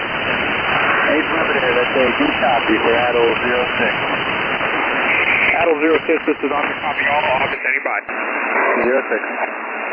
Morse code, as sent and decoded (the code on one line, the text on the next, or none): none